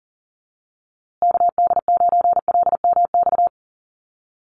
Morse code, heard as a keyed tone, about 40 wpm, 700 Hz. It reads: KD9LMX
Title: USA Call Signs Repeated 3x 40wpm